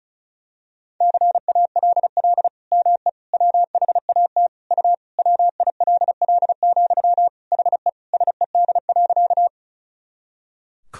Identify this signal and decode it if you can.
Morse code, heard as a keyed tone, about 34 wpm, 700 Hz. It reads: CALL ME WHAT U WILL, HE SED.
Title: Sentences from Top 200 Words and Abbreviations 35wpm